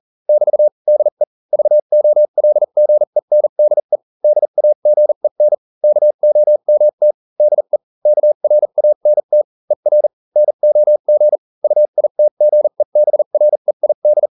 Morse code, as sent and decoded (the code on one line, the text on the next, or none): -...- / -.. / ...- --- .-.. --. -. -.. / -.. .- --. -. / -.- --- -- - / -.. / -.- .-. .- -. - / . .-. / -. --- --. / ..- .. - --. -... .-. .. -..
<BT> DE VOLGENDE DAGEN KOMT DE KRANT ER NOG UITGEBREID